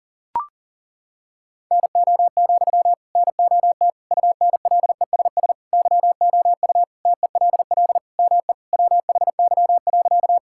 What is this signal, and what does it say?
Morse code, heard as a keyed tone, about 40 wpm, 700 Hz. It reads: NO, NOT UNLESS YOU TELL ME WHY.
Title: Sentences from Top 1000 Words - Review Part 6 - 40wpm